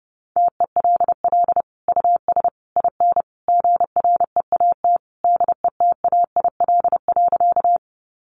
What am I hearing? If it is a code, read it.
Morse code, heard as a keyed tone, about 30 wpm, 700 Hz. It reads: TELL US IN GREAT DETAIL.
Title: Sentences from Top 1500 Words 30wpm